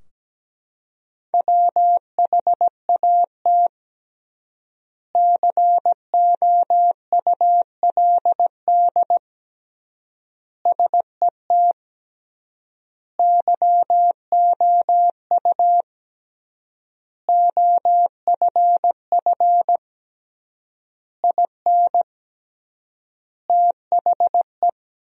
Morse code, heard as a keyed tone, about 17 wpm, 700 Hz. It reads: WHAT COULD SET YOU OFF IN THE
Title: 3x Word-Spacing - Sentences from Top 200 Words - 17wpm